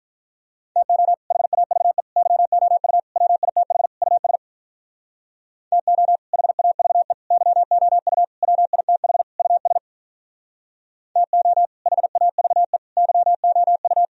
Morse code, heard as a keyed tone, about 44 wpm, 700 Hz. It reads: TO HAVE YOU WITH US TO HAVE YOU WITH US TO HAVE YOU
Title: Speed-Racing - Sets of 5 Words 35wpm